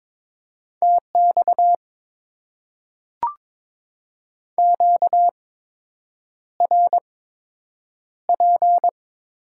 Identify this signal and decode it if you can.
Morse code, heard as a keyed tone, about 22 wpm, spaced as for 8 wpm, 700 Hz. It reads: TX QRP
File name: Top 90 QSO Elements - 8x Character Spacing - 22wpm